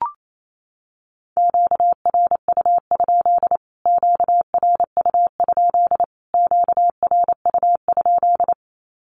Morse code, heard as a keyed tone, about 28 wpm, 700 Hz. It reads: QRU? QRU? QRU?
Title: Q-Codes Repeated 3x 28wpm